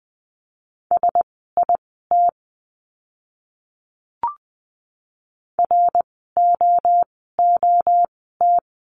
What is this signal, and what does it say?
Morse code, heard as a keyed tone, about 20 wpm, spaced as for 14 wpm, 700 Hz. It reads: SIT ROOT
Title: 054 Lesson 10f - Letter r - Words 20wpm